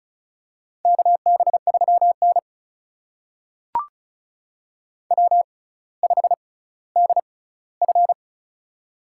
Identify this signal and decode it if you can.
Morse code, heard as a keyed tone, about 35 wpm, spaced as for 18 wpm, 700 Hz. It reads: KB3N W5DF
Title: USA Call Signs - 4 Characters - 6x Character Spacing - 35wpm